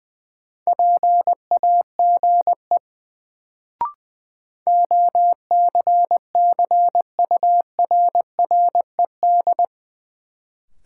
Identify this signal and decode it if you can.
Morse code, heard as a keyed tone, about 20 wpm, 700 Hz. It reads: PAGE OCCURRED
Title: Sets of 5 Words Encore - New Word Drill - 20wpm